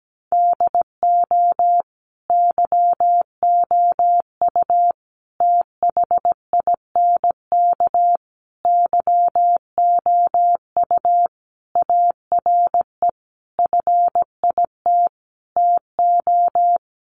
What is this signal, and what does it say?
Morse code, heard as a keyed tone, about 17 wpm, 700 Hz. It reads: DO YOU THINK YOU ARE FIT TO
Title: Sentences from Top 1200 Words 17wpm